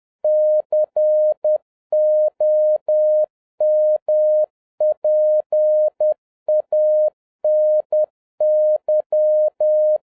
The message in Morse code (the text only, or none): COMPANY